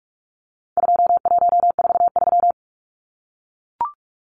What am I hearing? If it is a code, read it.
Morse code, heard as a keyed tone, about 44 wpm, 700 Hz. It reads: 2143
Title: Numbers - 4-Digit 45wpm